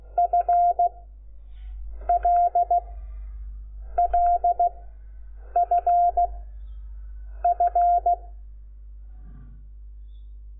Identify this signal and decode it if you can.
Morse code, heard as a keyed tone, about 16 wpm, 700 Hz. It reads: F L L F F
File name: Lecciones / CLASE Nº5